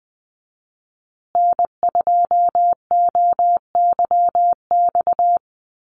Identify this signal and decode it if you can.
Morse code, heard as a keyed tone, about 19 wpm, 700 Hz. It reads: N2OYX
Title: USA Call Signs Repeated 3x 20wpm